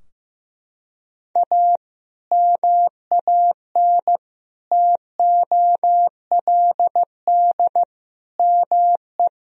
Morse code, heard as a keyed tone, about 16 wpm, 700 Hz. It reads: A MAN TOLD ME
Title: Sentences from Top 1000 Words - Review Part 3 - 15wpm